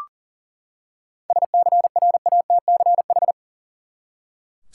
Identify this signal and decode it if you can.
Morse code, heard as a keyed tone, about 40 wpm, 700 Hz. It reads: SCRATCH
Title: Seven Letter Words - 40wpm